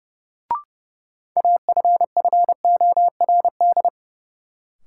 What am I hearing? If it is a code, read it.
Morse code, heard as a keyed tone, about 30 wpm, 700 Hz. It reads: AFFORD
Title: Sentences from Top 1100 Words - New Word Drill Alt - 30wpm